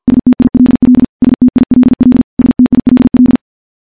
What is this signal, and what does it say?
Morse code, heard as a keyed tone, about 52 wpm, 250 Hz. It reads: STILL STILL STILL
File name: morse / morse-training-50WPM